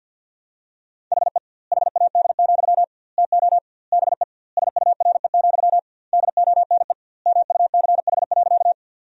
Morse code, heard as a keyed tone, about 52 wpm, 700 Hz. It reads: HE HAD, TO BE SURE, DONE MUCH.
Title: 2x Word-Spacing - Sentences from Top 500 Words - 50wpm